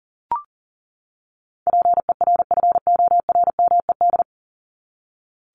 Morse code, heard as a keyed tone, about 37 wpm, 700 Hz. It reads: PERFORMED
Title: ED Suffix - Top 100 Words with Base Words 40wpm